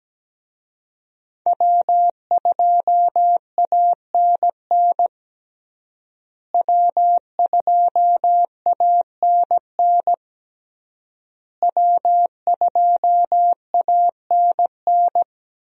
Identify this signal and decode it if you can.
Morse code, heard as a keyed tone, about 17 wpm, 700 Hz. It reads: W2ANN W2ANN W2ANN
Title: USA Call Signs - 5 Characters - Repeated 3x - 17wpm